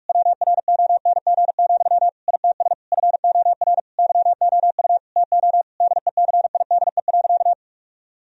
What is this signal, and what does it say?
Morse code, heard as a keyed tone, about 46 wpm, 700 Hz. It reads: WRONG, ITS FOR YOU TO DECIDE.
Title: Sentences from Top 1000 Words - Review Part 3 - 45wpm